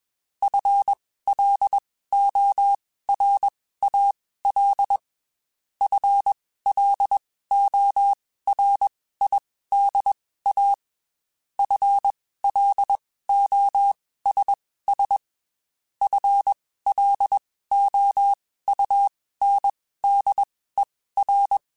Morse code, heard as a keyed tone, about 21 wpm, spaced as for 15 wpm, 800 Hz. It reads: FLORAL FLORIDA FLOSS FLOUNDER
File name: Spaced / FL Runs sp mp3s / FL Runs sp @20